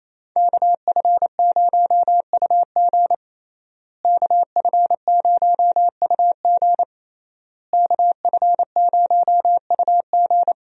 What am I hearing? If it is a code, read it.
Morse code, heard as a keyed tone, about 28 wpm, 700 Hz. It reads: KF0UG KF0UG KF0UG
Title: USA Call Signs - 5 Characters - Repeated 3x - 28wpm